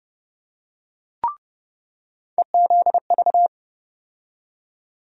Morse code, heard as a keyed tone, about 30 wpm, 700 Hz. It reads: EZV
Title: USA Call Sign Postfixes - 3 Letters 30wpm